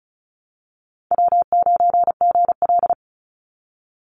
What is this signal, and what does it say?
Morse code, heard as a keyed tone, about 34 wpm, 700 Hz. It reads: W9GL
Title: Call Signs Repeated 3x - 35wpm